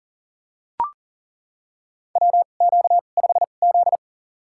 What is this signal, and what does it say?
Morse code, heard as a keyed tone, about 40 wpm, spaced as for 29 wpm, 700 Hz. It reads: WQ5Z
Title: USA Call Signs - 4 Characters - 2x Character Spacing - 40wpm